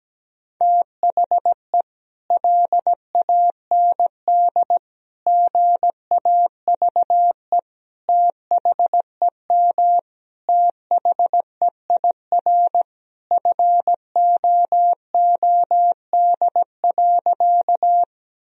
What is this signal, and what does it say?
Morse code, heard as a keyed tone, about 17 wpm, 700 Hz. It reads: THE LAND GAVE THEM THEIR FOOD.
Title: Sentences from Top 1000 Words - Review Part 1 - 17wpm